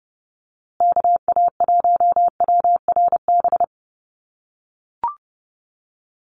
Morse code, heard as a keyed tone, about 30 wpm, 700 Hz. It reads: KA1WRB
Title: USA Call Signs Repeated 3x 30wpm